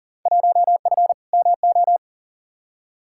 Morse code, heard as a keyed tone, about 40 wpm, 700 Hz. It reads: 1F MO
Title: Field Day Exchanges Encore - 40wpm